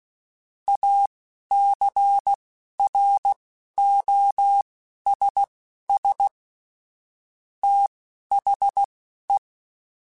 Morse code, heard as a keyed tone, about 16 wpm, spaced as for 11 wpm, 800 Hz. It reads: ACROSS THE